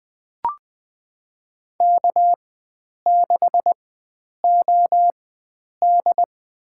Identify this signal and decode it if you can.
Morse code, heard as a keyed tone, about 20 wpm, spaced as for 9 wpm, 700 Hz. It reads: K6OD
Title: USA Call Signs - 4 Characters - 4x Character Spacing - 20wpm